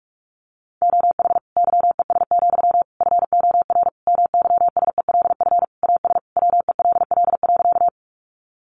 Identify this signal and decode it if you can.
Morse code, heard as a keyed tone, about 44 wpm, 700 Hz. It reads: OH YES, FOR MYSELF AS WELL.
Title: Sentences from Top 1300 Words 45wpm